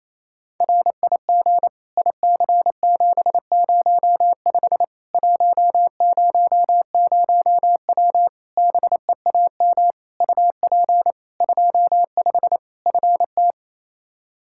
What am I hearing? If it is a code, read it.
Morse code, heard as a keyed tone, about 28 wpm, 700 Hz. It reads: RIG IC705 100W BEAM UP 25 FT
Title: K1USN SST QSO Practice 28wpm